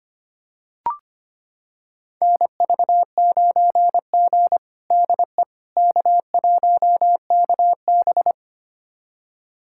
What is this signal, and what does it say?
Morse code, heard as a keyed tone, about 25 wpm, 700 Hz. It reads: NV9G DE K1KB
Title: USA Callsign DE Callsign - 4 Characters - 25wpm